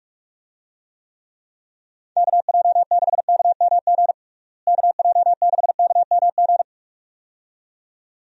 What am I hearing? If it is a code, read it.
Morse code, heard as a keyed tone, about 46 wpm, 700 Hz. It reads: KJ6KMG KJ6KMG
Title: USA Call Signs - 6 Characters - Repeated 2x - 45wpm